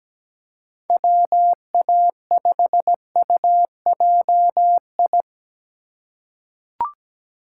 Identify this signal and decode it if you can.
Morse code, heard as a keyed tone, about 17 wpm, 700 Hz. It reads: WA5UJI
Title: USA Call Signs - 6 Characters - 2x Character Spacing - 17wpm